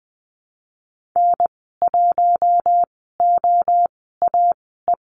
Morse code, heard as a keyed tone, about 20 wpm, spaced as for 15 wpm, 700 Hz. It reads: N1OAE
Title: 055 Lesson 10g - Letter r - Call Signs 20wpm